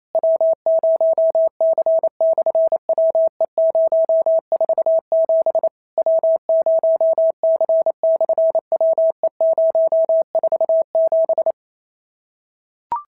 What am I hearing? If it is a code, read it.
Morse code, heard as a keyed tone, about 28 wpm, 650 Hz. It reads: W0C/WE047 W0C/WE047
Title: SOTA QSO Practice 28wpm